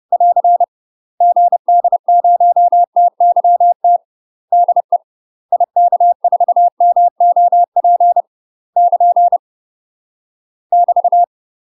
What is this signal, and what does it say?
Morse code, heard as a keyed tone, about 30 wpm, 700 Hz. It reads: <AR> GD0TYT DE IK4MOP <KN> <BT>
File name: UK MASTER QSOs TRAINING / 30WPM / UK MASTER QSOs 0008